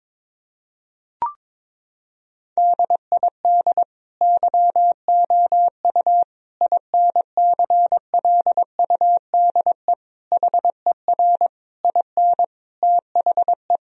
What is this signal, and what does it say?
Morse code, heard as a keyed tone, about 22 wpm, 700 Hz. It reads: DID YOU INCLUDE HER IN THE
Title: Sentences from Top 1000 Words - Review Part 3 - 22wpm